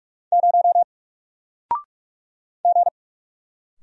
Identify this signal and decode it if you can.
Morse code, heard as a keyed tone, about 44 wpm, 700 Hz. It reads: 0 G
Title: Single Letter-Number Alt 45wpm